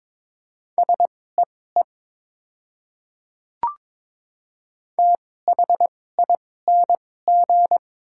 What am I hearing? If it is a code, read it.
Morse code, heard as a keyed tone, about 22 wpm, spaced as for 16 wpm, 700 Hz. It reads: SEE THING
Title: 2x Character Spacing - Top 100 Words - 22wpm